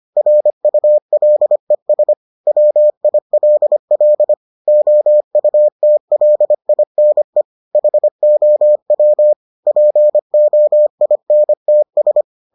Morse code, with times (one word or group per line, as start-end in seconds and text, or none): 0.2-2.1 RULES
2.5-4.3 WILL
4.7-7.4 OUTLINE
7.8-9.3 HOW
9.7-12.2 POINTS